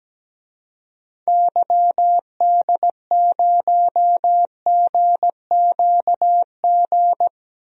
Morse code, heard as a keyed tone, about 17 wpm, 700 Hz. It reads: YD0GQG
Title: DXCC-100 Call Signs - 6 Characters - 17wpm